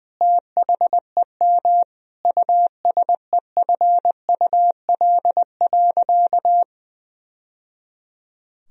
Morse code, heard as a keyed tone, about 21 wpm, 700 Hz. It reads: THEM USEFUL.
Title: Sentences from Top 1000 Words - Review Part 4 - 20wpm